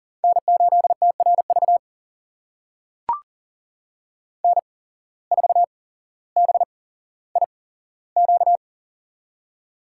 Morse code, heard as a keyed tone, about 40 wpm, spaced as for 17 wpm, 700 Hz. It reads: N8TRV N4BIQ
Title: USA Call Signs - 5 Characters - 8x Character Spacing - 40wpm